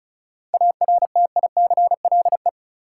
Morse code, heard as a keyed tone, about 36 wpm, 700 Hz. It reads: ARTICLE
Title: Sentences from Top 1000 Words - Review Part 5 - New Word Drill - 35wpm